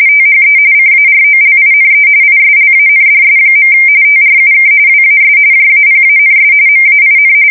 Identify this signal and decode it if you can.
RTTY, about 45.45 baud. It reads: THE CAROLINE ISLANDS. TRUK WAS JAPANS MAIN
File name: fsk170 / battleshipnj